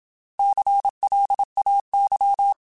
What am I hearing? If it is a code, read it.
Morse code, heard as a keyed tone, about 27 wpm, 800 Hz. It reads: CLAY
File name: Alphabetical / Alphabetical mp3s / Cs @25